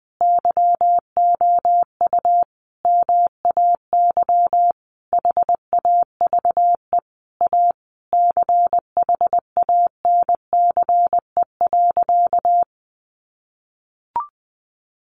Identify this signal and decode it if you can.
Morse code, heard as a keyed tone, about 20 wpm, 700 Hz. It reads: YOU MAY HAVE A CHANCE.
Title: Sentences from Top 1400 Words 20wpm